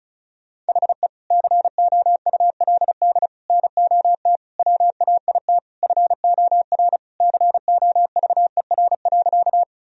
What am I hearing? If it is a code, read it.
Morse code, heard as a keyed tone, about 36 wpm, 690 Hz. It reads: HE COULD NOT WAIT FOR COVER.
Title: Sentences from Top 1000 Words - Review Part 1 - 35wpm